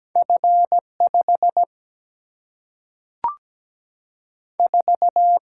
Morse code, heard as a keyed tone, about 17 wpm, 700 Hz. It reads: F5 4
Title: Single-Letter Single-Number 17wpm